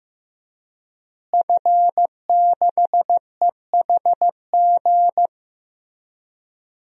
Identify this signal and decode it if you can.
Morse code, heard as a keyed tone, about 15 wpm, 700 Hz. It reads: F6EHG
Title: DXCC-100 Call Signs - 5 Characters - Repeated 3x - 15wpm